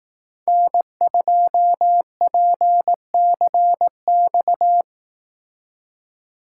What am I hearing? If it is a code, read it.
Morse code, heard as a keyed tone, about 18 wpm, 700 Hz. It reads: N2PCX
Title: Speed Racing - USA 1x3 Call Signs 15wpm